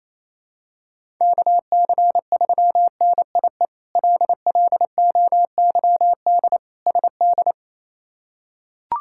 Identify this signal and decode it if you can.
Morse code, heard as a keyed tone, about 27 wpm, 700 Hz. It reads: KC3NIE LLOYD SD
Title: K1USN SST QSO Practice - 28wpm